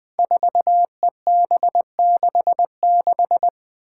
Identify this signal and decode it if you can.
Morse code, heard as a keyed tone, about 20 wpm, 700 Hz. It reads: 4EB66
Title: Five Challenging Letter-Number Combinations 20wpm